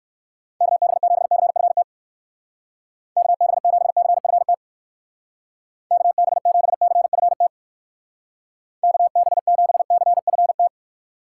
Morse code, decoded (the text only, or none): KB7CFT KB7CFT KB7CFT KB7CFT